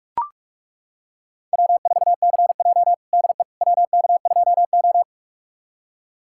Morse code, heard as a keyed tone, about 42 wpm, 700 Hz. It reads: W4CJ DE WK2O
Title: USA Callsign DE Callsign - 4 Characters - 45wpm